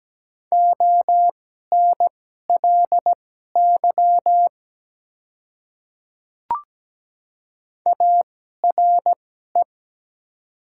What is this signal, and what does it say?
Morse code, heard as a keyed tone, about 17 wpm, spaced as for 12 wpm, 700 Hz. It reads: ONLY ARE
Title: 2x Character Spacing - Top 200 Words - 17wpm